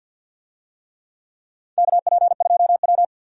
Morse code, heard as a keyed tone, about 46 wpm, 700 Hz. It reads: KP2W
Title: USA Call Signs - 4 Characters - 50wpm